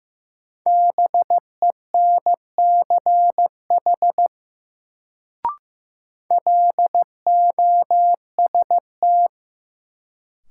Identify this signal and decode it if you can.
Morse code, heard as a keyed tone, about 15 wpm, 700 Hz. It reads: BENCH LOST
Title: Sentences from Top 1400 Words - New Word Drill Alt - 15wpm